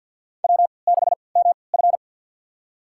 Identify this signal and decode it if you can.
Morse code, heard as a keyed tone, about 51 wpm, spaced as for 28 wpm, 700 Hz. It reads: W6MF
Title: USA Call Signs - 4 Characters - 3x Character Spacing - 50wpm